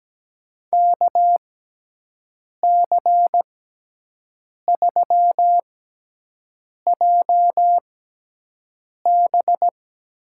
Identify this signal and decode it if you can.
Morse code, heard as a keyed tone, about 17 wpm, spaced as for 6 wpm, 700 Hz. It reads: KC3JB
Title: USA Call Signs - 6 Characters - 6x Character Spacing - 17wpm